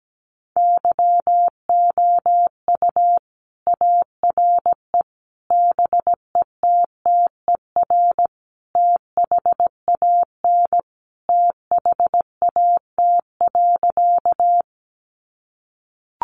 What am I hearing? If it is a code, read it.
Morse code, heard as a keyed tone, about 17 wpm, 700 Hz. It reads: YOU ARE BETTER THAN THAT.
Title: Sentences from Top 400 Words 17wpm